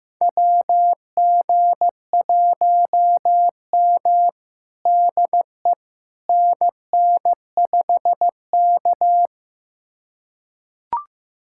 Morse code, heard as a keyed tone, about 15 wpm, 700 Hz. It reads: WG1M DE NN5K
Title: USA Callsign DE Callsign - 4 Characters - 15wpm